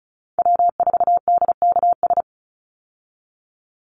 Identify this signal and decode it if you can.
Morse code, heard as a keyed tone, about 34 wpm, 700 Hz. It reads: W4DKS
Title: Call Signs Repeated 3x - 35wpm